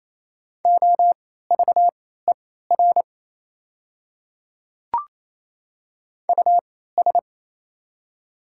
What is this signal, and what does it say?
Morse code, heard as a keyed tone, about 27 wpm, spaced as for 16 wpm, 700 Hz. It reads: OVER US
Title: 3x Character Spacing - Top 200 Words - 28wpm